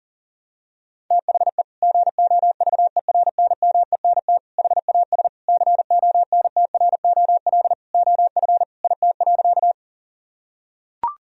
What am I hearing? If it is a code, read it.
Morse code, heard as a keyed tone, about 37 wpm, 700 Hz. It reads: THE GOVERNMENT HAS CONTROL OF IT.
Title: Sentences from Top 1000 Words - Review Part 3 - 40wpm